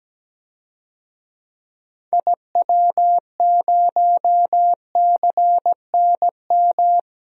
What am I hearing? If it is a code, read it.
Morse code, heard as a keyed tone, about 17 wpm, 700 Hz. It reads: IW0CNM
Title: DXCC-100 Call Signs - 6 Characters - 17wpm